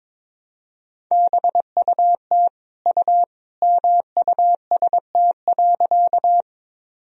Morse code, heard as a keyed tone, about 22 wpm, 700 Hz. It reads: BUT U MUST.
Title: Sentences from Top 200 Words and Abbreviations 22wpm